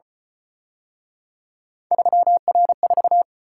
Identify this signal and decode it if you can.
Morse code, heard as a keyed tone, about 32 wpm, 700 Hz. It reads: 3R4